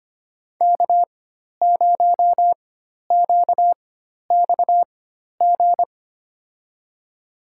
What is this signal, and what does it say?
Morse code, heard as a keyed tone, about 25 wpm, spaced as for 12 wpm, 700 Hz. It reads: K0QXG
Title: USA Call Signs - 5 Characters - 4x Character Spacing - 25wpm